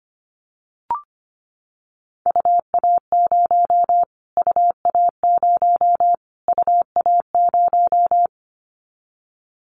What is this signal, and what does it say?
Morse code, heard as a keyed tone, about 25 wpm, 700 Hz. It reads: UA0 UA0 UA0
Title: Callsign Prefixes for Top 25 DXCC - Repeated 3x - 25wpm